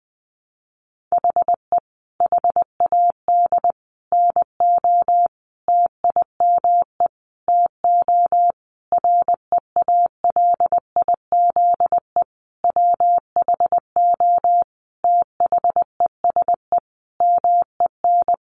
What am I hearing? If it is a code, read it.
Morse code, heard as a keyed tone, about 20 wpm, 700 Hz. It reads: HE HAD NO TIME TO REALIZE WHO THESE MEN
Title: Sentences from Top 1000 Words 20wpm